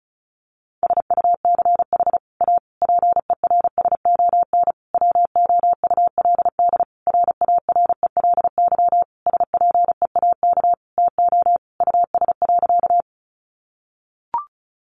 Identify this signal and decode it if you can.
Morse code, heard as a keyed tone, about 34 wpm, 700 Hz. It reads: SUCH A PERSON WOULD RARELY SPEAK TO US.
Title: Sentences from Top 1400 Words 35wpm